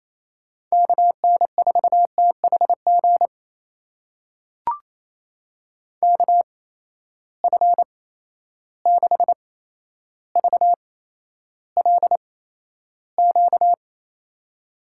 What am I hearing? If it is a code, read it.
Morse code, heard as a keyed tone, about 28 wpm, spaced as for 12 wpm, 700 Hz. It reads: KN4THG KF6VLQ
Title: USA Call Signs - 6 Characters - 8x Character Spacing - 28wpm